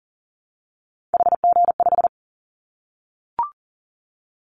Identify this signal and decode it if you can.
Morse code, heard as a keyed tone, about 40 wpm, 700 Hz. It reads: HG5
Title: Callsign Prefixes for Top 25 DXCC - 40wpm